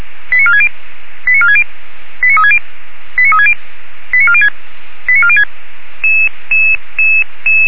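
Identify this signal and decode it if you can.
Five-tone selective call (ZVEI), heard as quick runs of five tones, tones of about 70 ms. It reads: 89460 89460 89360 89360 89406 89406
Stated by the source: ZVEI five-tone selective call 89466 received by SDR.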